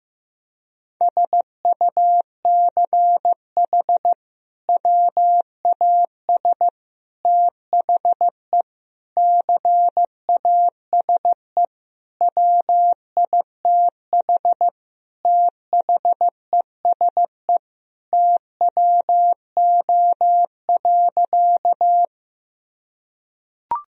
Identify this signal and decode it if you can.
Morse code, heard as a keyed tone, about 16 wpm, 700 Hz. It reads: SUCH WAS THE CASE WITH THESE TWO.
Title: Sentences from Top 1000 Words - Review Part 1 - 15wpm